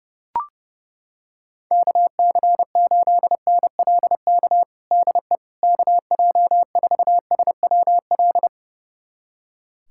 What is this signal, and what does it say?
Morse code, heard as a keyed tone, about 30 wpm, 700 Hz. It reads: KC8NLK DE KJ4SWL
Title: USA Callsign DE Callsign 30wpm